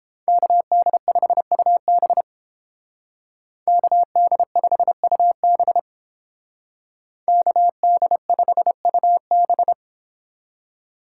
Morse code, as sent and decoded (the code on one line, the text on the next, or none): -.- -.. ..... ..- -... / -.- -.. ..... ..- -... / -.- -.. ..... ..- -...
KD5UB KD5UB KD5UB